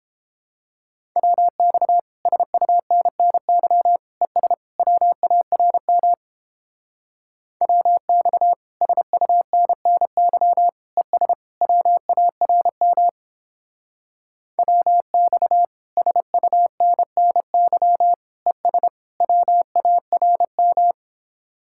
Morse code, heard as a keyed tone, about 29 wpm, 700 Hz. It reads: WX SUNNY ES WARM WX SUNNY ES WARM WX SUNNY ES WARM